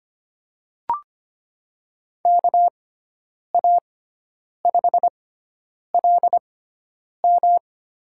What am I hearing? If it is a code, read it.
Morse code, heard as a keyed tone, about 25 wpm, spaced as for 9 wpm, 700 Hz. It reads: KA5LM
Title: USA Call Signs - 6 Characters - 6x Character Spacing - 25wpm